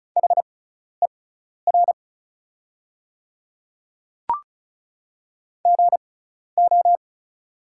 Morse code, heard as a keyed tone, about 36 wpm, spaced as for 12 wpm, 700 Hz. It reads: HER GO
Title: 6x Character Spacing - Top 100 Words - 35wpm